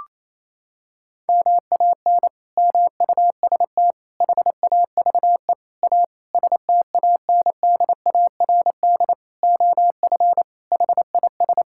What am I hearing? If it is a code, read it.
Morse code, heard as a keyed tone, about 28 wpm, 700 Hz. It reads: MAN MUST HAVE A STANDARD OF HIS
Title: Sentences from Top 1000 Words - Review Part 6 - 28wpm